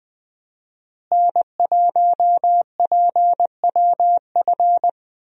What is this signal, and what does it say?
Morse code, heard as a keyed tone, about 20 wpm, 700 Hz. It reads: N1PWF
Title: USA Call Signs - 5 Characters - Repeated 3x - 20wpm